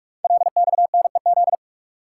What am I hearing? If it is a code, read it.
Morse code, heard as a keyed tone, about 44 wpm, 690 Hz. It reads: LXNEZ
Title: Code Groups - 5 Letters - 45wpm